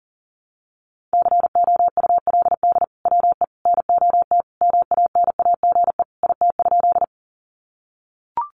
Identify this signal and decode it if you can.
Morse code, heard as a keyed tone, about 40 wpm, 700 Hz. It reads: COULD WE NOT MANAGE IT?
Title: Sentences from Top 1400 Words 40wpm